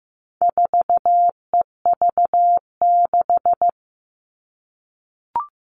Morse code, heard as a keyed tone, about 15 wpm, 700 Hz. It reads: 4EV6
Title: All Cut Numbers - 4-Digit 15wpm